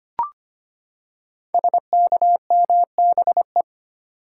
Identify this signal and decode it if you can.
Morse code, heard as a keyed tone, about 25 wpm, 690 Hz. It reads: SKMBE
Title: Code Groups - 5 Letters - 25wpm